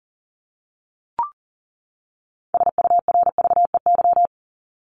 Morse code, heard as a keyed tone, about 40 wpm, 700 Hz. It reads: SURVEY
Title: Top 1501-1750 Words 40wpm